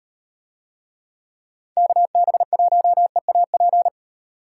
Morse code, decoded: KB1EAP

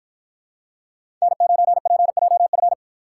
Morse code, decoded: N9PJF

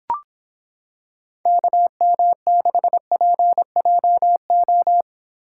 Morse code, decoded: KM6PJO

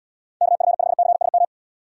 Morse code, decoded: KF5CSM

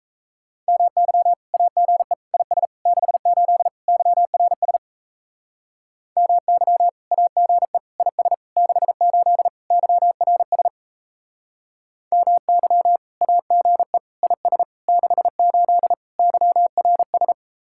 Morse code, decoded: MY AGE IS 68 YRS MY AGE IS 68 YRS MY AGE IS 68 YRS

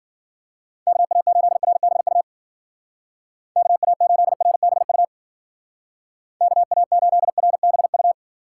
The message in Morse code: KA8RBU KA8RBU KA8RBU

